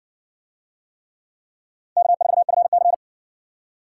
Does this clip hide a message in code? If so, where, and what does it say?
Yes: Morse code from 2.0 to 2.9 s, heard as a keyed tone, about 60 wpm, 700 Hz. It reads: K4FC